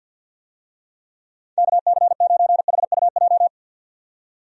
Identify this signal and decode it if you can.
Morse code, heard as a keyed tone, about 50 wpm, 700 Hz. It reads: KC9HUJ
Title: USA Call Signs - 6 Characters - 50wpm